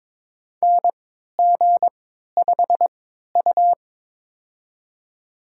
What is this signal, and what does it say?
Morse code, heard as a keyed tone, about 22 wpm, spaced as for 13 wpm, 700 Hz. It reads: NG5U
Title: USA Call Signs - 4 Characters - 3x Character Spacing - 22wpm